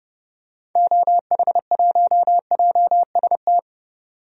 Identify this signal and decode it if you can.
Morse code, heard as a keyed tone, about 30 wpm, 700 Hz. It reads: OH1JST
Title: DXCC-100 Call Signs - 6 Characters - 30wpm